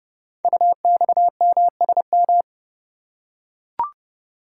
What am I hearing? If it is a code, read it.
Morse code, heard as a keyed tone, about 30 wpm, 700 Hz. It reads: UXMSM
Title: Code Groups - 5 Letters - 30wpm